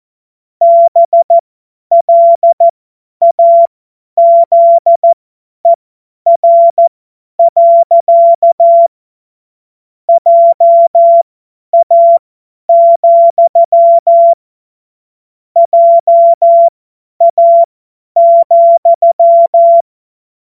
Morse code, heard as a keyed tone, about 14 wpm, spaced as for 10 wpm, 680 Hz. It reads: BLAZER. JA, JA,
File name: Morsekurs / Witze / Witz 16 Januar 2024 50 F70